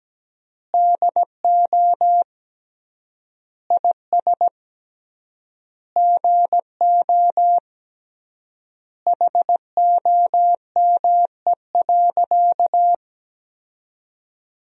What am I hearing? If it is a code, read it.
Morse code, heard as a keyed tone, about 17 wpm, 700 Hz. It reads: DO IS GO HOME.
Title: 3x Word-Spacing - Sentences from Top 200 Words - 17wpm